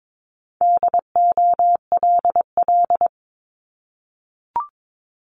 Morse code, heard as a keyed tone, about 22 wpm, 700 Hz. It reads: DOLL
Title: Multi-letter Phonemes 22wpm